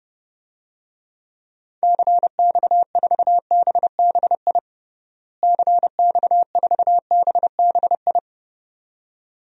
Morse code, heard as a keyed tone, about 30 wpm, 700 Hz. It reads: CX4BBI CX4BBI
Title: DXCC-100 Call Signs - 6 Characters - Repeated 2x - 30wpm